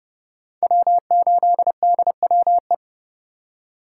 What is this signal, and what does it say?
Morse code, heard as a keyed tone, about 30 wpm, 700 Hz. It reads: W8DWE
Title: USA Call Signs - 5 Characters - 2x Character Spacing - 30wpm